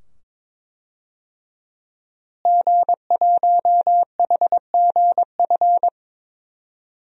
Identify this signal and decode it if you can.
Morse code, heard as a keyed tone, about 22 wpm, 700 Hz. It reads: G1HGF
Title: DXCC-100 Call Signs - 5 Characters - Repeated 2x - 22wpm